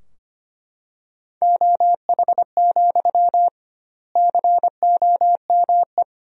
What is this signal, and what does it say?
Morse code, heard as a keyed tone, about 25 wpm, 700 Hz. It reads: OH, COME
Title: 2x Word-Spacing - Sentences from Top 500 Words - 25wpm